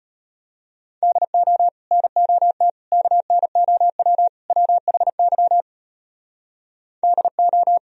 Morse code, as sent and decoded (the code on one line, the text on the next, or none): -.. --- / -. --- - / -.- -. --- .-- / .-- .... -.-- / -.. ---
DO NOT KNOW WHY DO